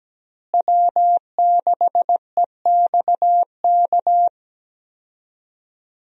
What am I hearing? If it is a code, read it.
Morse code, heard as a keyed tone, about 17 wpm, 700 Hz. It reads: W6EXK
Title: USA Call Signs - 5 Characters - Repeated 2x - 17wpm